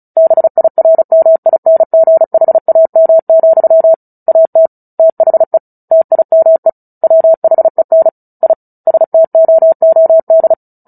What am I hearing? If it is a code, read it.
Morse code, heard as a keyed tone, about 34 wpm, 650 Hz. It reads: BIRMINGHAM, AT THE TIME WHEN I STOOD
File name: Book Two / 09-Chapter 9 - 35 WPM